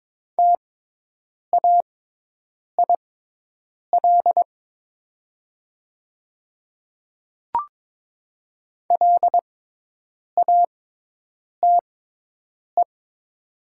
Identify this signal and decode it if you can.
Morse code, heard as a keyed tone, about 22 wpm, spaced as for 8 wpm, 700 Hz. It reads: TAIL LATE
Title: 6x Character Spacing - Top 500 Words - 22wpm